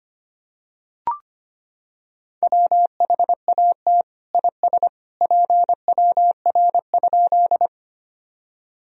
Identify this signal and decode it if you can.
Morse code, heard as a keyed tone, about 25 wpm, 700 Hz. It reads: WHAT IS PWR?
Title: Sentences from Top 500 Words and Abbreviations 25wpm